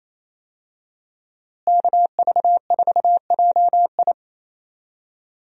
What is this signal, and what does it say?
Morse code, heard as a keyed tone, about 28 wpm, 700 Hz. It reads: KV4JI
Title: USA Call Signs - 5 Characters - Repeated 3x - 28wpm